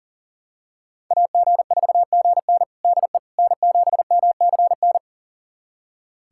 Morse code, heard as a keyed tone, about 37 wpm, 700 Hz. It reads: AG4GN DE N7MCN
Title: USA Callsign DE Callsign - 5 Characters - 40wpm